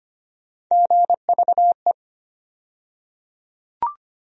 Morse code, heard as a keyed tone, about 25 wpm, 700 Hz. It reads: GVE
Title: USA Call Sign Postfixes - 3 Letters 25wpm